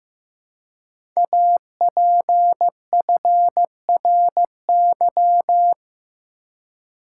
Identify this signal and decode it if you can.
Morse code, heard as a keyed tone, about 15 wpm, 700 Hz. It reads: APFRY
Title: Code Groups - 5 Letters - 15wpm